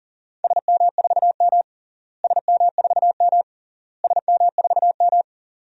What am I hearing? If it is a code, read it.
Morse code, heard as a keyed tone, about 40 wpm, 700 Hz. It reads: SM4M SM4M SM4M
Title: DXCC-100 Call Signs - 4 Characters - Repeated 3x - 40wpm